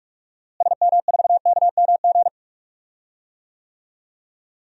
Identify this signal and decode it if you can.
Morse code, heard as a keyed tone, about 41 wpm, 700 Hz. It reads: SM4KMG
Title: DXCC-100 Call Signs - 6 Characters - Repeated 3x - 45wpm